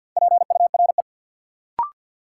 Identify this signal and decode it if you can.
Morse code, heard as a keyed tone, about 52 wpm, 700 Hz. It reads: PURE
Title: Sentences from Top 1500 Words - New Word Drill Alt - 50wpm